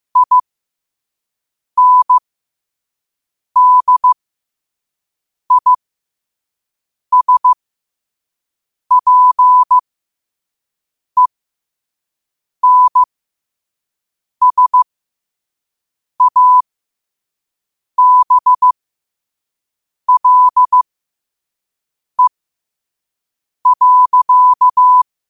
Morse code, heard as a keyed tone, about 15 wpm, spaced as for 5 wpm, 1000 Hz. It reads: INDISPENSABLE.
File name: qotdfiles / 5wpm